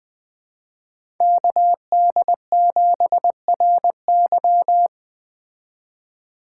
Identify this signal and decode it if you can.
Morse code, heard as a keyed tone, about 20 wpm, 700 Hz. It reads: KD7RY